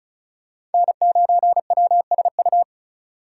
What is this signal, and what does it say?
Morse code, heard as a keyed tone, about 34 wpm, 700 Hz. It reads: N9WSU
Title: USA Call Signs - 5 Characters - 3x Character Spacing - 35wpm